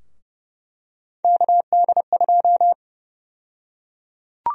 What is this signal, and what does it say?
Morse code, heard as a keyed tone, about 29 wpm, 700 Hz. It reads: KD2
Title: USA Call Sign Prefixes - 30wpm